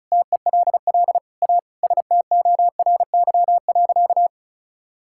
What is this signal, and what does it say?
Morse code, heard as a keyed tone, about 34 wpm, 700 Hz. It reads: TELL A STORY.
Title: Sentences from Top 1000 Words - Review Part 6 - 35wpm